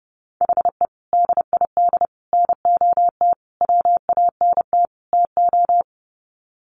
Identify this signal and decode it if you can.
Morse code, heard as a keyed tone, about 30 wpm, 700 Hz. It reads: HE DID NOT WANT TO
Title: Sets of 5 Words 30wpm